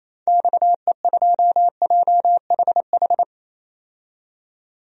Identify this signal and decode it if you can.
Morse code, heard as a keyed tone, about 28 wpm, 700 Hz. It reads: XE2JHH
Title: DXCC-100 Call Signs - 6 Characters - 28wpm